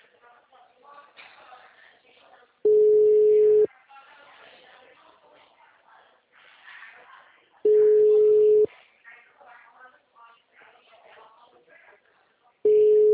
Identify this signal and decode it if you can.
Ringback tone (Europe, 425 Hz), with speech.